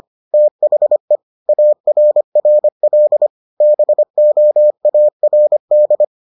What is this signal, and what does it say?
Morse code, heard as a keyed tone, about 25 wpm, 600 Hz. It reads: THE ARRL BOARD